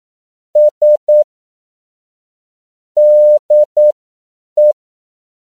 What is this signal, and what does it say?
Morse code, heard as a keyed tone, about 9 wpm, spaced as for 7 wpm, 600 Hz. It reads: S DE
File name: CW / 10PPM